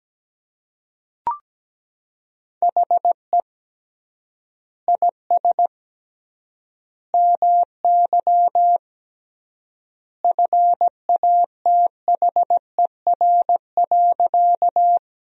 Morse code, heard as a keyed tone, about 17 wpm, 700 Hz. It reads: HE IS MY FATHER.
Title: 3x Word-Spacing - Sentences from Top 300 Words - 17wpm